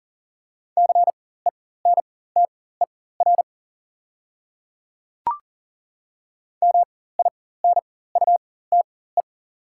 Morse code, heard as a keyed tone, about 37 wpm, spaced as for 18 wpm, 690 Hz. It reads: CENTER MINUTE
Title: 4x Character Spacing - Top 500 Words - 40wpm